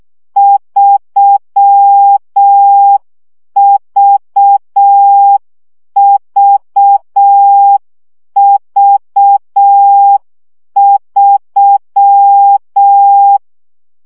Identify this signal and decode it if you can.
Morse code, heard as a keyed tone, about 6 wpm, 800 Hz. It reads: 3VVV3
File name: CW gyakorlas v 3 5 baud